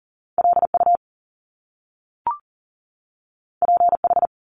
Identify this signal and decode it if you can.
Morse code, heard as a keyed tone, about 39 wpm, 700 Hz. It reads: LU PH
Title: Top Two Letter Combinations in Words 40wpm